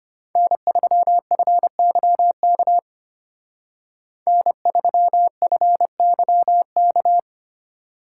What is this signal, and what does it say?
Morse code, heard as a keyed tone, about 33 wpm, 700 Hz. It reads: N3FYK N3FYK